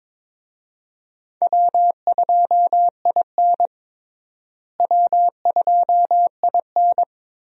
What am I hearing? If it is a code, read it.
Morse code, heard as a keyed tone, about 22 wpm, 700 Hz. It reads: W2IN W2IN
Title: USA Call Signs - 4 Characters - Repeated 3x - 22wpm